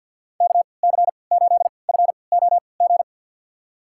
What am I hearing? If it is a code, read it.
Morse code, heard as a keyed tone, about 51 wpm, spaced as for 28 wpm, 700 Hz. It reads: KC8FOG
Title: USA Call Signs - 6 Characters - 3x Character Spacing - 50wpm